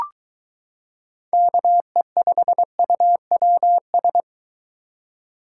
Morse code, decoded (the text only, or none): KE5UWS